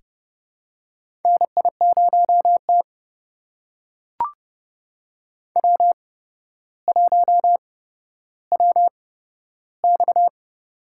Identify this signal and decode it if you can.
Morse code, heard as a keyed tone, about 30 wpm, spaced as for 13 wpm, 700 Hz. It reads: NI0T W1WX
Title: USA Call Signs - 4 Characters - 8x Character Spacing - 30wpm